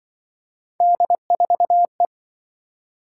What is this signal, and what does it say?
Morse code, heard as a keyed tone, about 26 wpm, 700 Hz. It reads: D4E